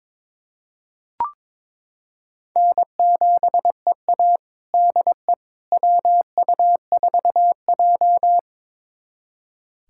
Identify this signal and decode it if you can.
Morse code, heard as a keyed tone, about 22 wpm, 700 Hz. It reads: N7EA DE WU4J
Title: USA Callsign DE Callsign - 4 Characters - 22wpm